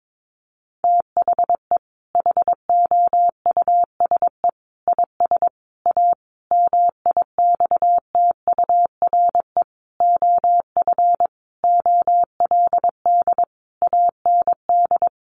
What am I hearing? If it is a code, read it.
Morse code, heard as a keyed tone, about 22 wpm, 700 Hz. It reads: THE HOUSE IS A MIXTURE OF OLD AND
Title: Sentences from Top 1500 Words 22wpm